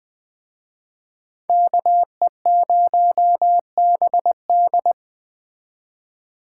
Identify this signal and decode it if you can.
Morse code, heard as a keyed tone, about 20 wpm, 700 Hz. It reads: KE0BD
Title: Speed Racing - USA 2x2 Call Signs 17wpm